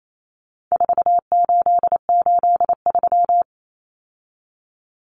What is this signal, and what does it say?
Morse code, heard as a keyed tone, about 28 wpm, 700 Hz. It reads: 4883
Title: Numbers - 4-Digit 28wpm